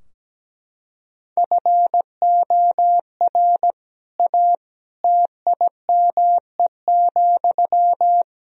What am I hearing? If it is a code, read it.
Morse code, heard as a keyed tone, about 17 wpm, 700 Hz. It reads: FOR A TIME,
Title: Sentences from Top 1000 Words - Review Part 4 - 17wpm